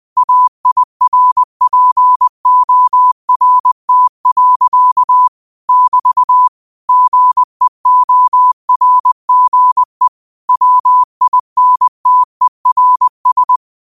Morse code, heard as a keyed tone, about 20 wpm, 1000 Hz. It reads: AIRPORT. <BT> GEORGE WINTERS
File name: qotdfiles / 20wpm